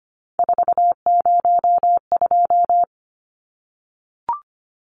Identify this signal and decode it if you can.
Morse code, heard as a keyed tone, about 25 wpm, 700 Hz. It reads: 402
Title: Numbers - 3-Digit 25wpm